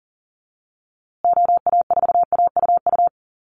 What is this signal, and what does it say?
Morse code, heard as a keyed tone, about 40 wpm, 700 Hz. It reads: OA4AUU
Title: USA Call Signs Repeated 2x 40wpm